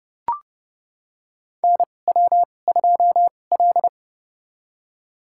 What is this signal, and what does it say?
Morse code, heard as a keyed tone, about 30 wpm, spaced as for 22 wpm, 700 Hz. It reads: NW2L
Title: USA Call Signs - 2x Character Spacing - 30wpm